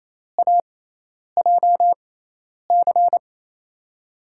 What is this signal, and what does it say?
Morse code, heard as a keyed tone, about 28 wpm, spaced as for 10 wpm, 700 Hz. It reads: AJC
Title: USA Call Signs - 5 Characters - 6x Character Spacing - 28wpm